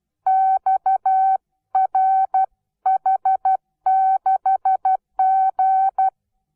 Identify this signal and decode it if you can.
Morse code, heard as a keyed tone, about 12 wpm, 760 Hz. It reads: XRH6G
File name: Onlinelehrgang / morsen / DJ4UF Morsen17